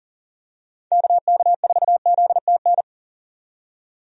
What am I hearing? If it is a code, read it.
Morse code, heard as a keyed tone, about 37 wpm, 700 Hz. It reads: KK4ZTN
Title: USA Call Signs - 6 Characters - 40wpm